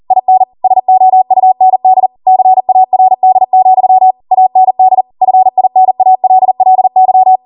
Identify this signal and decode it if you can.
Morse code, heard as a keyed tone, about 40 wpm, 750 Hz. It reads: IN SOUND CARD, AND FINALLY